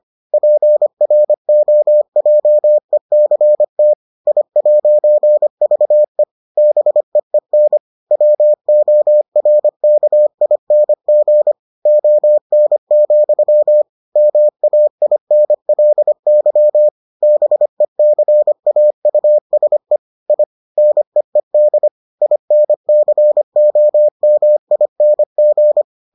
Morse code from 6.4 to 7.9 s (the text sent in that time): BEEN